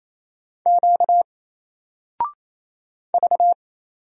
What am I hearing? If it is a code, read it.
Morse code, heard as a keyed tone, about 28 wpm, 700 Hz. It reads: Q V
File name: Single-Letter-Number-28wpm